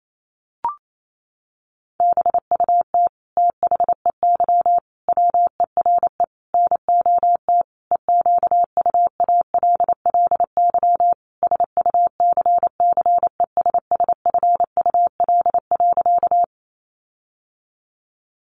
Morse code, heard as a keyed tone, about 28 wpm, 700 Hz. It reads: BUT THEY WERE NOT EQUALLY SUCCESSFUL.
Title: Sentences from Top 1400 Words 28wpm